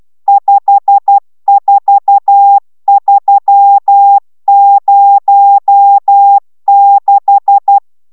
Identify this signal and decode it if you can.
Morse code, heard as a keyed tone, about 12 wpm, 800 Hz. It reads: 54306